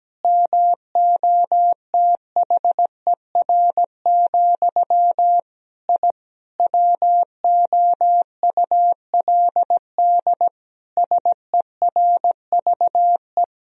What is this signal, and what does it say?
Morse code, heard as a keyed tone, about 17 wpm, 700 Hz. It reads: MOTHER, I WOULD SERVE
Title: Sentences from Top 1000 Words - Review Part 2 - 17wpm